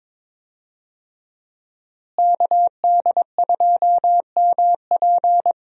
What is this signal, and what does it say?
Morse code, heard as a keyed tone, about 22 wpm, 700 Hz. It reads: KD2MP
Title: USA Call Signs - 5 Characters - Repeated 2x - 22wpm